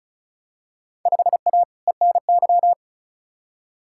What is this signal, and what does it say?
Morse code, heard as a keyed tone, about 36 wpm, 690 Hz. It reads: 5A ENY
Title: Field Day Exchanges Encore - 35wpm